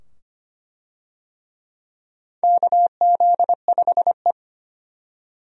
Morse code, heard as a keyed tone, about 25 wpm, 700 Hz. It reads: KZ5E
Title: USA Call Signs - 4 Characters - 25wpm